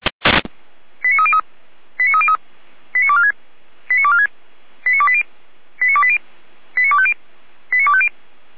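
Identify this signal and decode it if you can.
Five-tone selective call (ZVEI), heard as quick runs of five tones, tones of about 70 ms. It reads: 89303 89303 89246 89246 89280 89280 89250 89250
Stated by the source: ZVEI five-tone selective call received on SDR.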